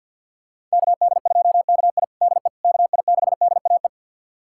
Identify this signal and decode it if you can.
Morse code, heard as a keyed tone, about 51 wpm, 700 Hz. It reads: KD2KI DE KI6DAE
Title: USA Callsign DE Callsign 50wpm